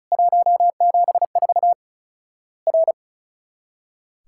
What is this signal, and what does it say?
Morse code, heard as a keyed tone, about 34 wpm, 700 Hz. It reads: 174 R
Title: Worked All Europe DX Contest - QTC Exchange - 35wpm